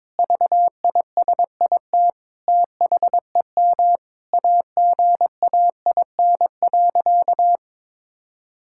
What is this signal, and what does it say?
Morse code, heard as a keyed tone, about 22 wpm, 700 Hz. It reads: VISIT THEM AGAIN.
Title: Sentences from Top 1000 Words - Review Part 6 - 22wpm